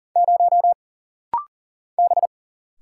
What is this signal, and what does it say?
Morse code, heard as a keyed tone, about 40 wpm, 700 Hz. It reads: 0 B
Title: Single Letter-Number Alt - Rapid-Fire 40wpm